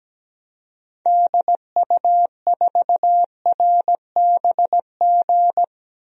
Morse code, heard as a keyed tone, about 17 wpm, 700 Hz. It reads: DU4RBG
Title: DXCC-100 Call Signs - 6 Characters - Repeated 3x - 17wpm